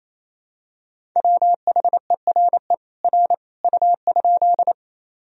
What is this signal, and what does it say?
Morse code, heard as a keyed tone, about 28 wpm, 700 Hz. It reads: WHERE R U?
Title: Sentences from Top 200 Words and Abbreviations 28wpm